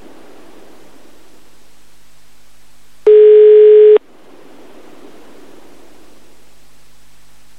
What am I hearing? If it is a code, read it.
Ringback tone (Europe, 425 Hz).